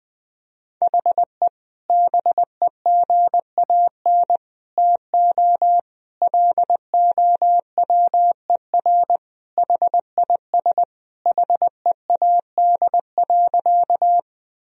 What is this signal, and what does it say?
Morse code, heard as a keyed tone, about 21 wpm, 700 Hz. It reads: HE BEGAN TO LOWER HIS HEAD.
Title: Sentences from Top 1000 Words - Review Part 4 - 20wpm